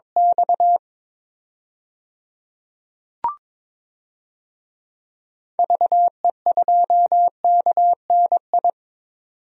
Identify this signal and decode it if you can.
Morse code, heard as a keyed tone, about 22 wpm, 700 Hz. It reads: X VE2KNI
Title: DXCC-100 Call Signs - 6 Characters - 22wpm